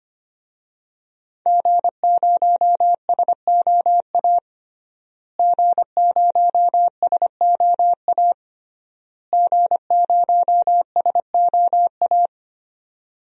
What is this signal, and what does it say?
Morse code, heard as a keyed tone, about 25 wpm, 700 Hz. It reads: G0SOA G0SOA G0SOA
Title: DXCC-100 Call Signs - 5 Characters - Repeated 3x - 25wpm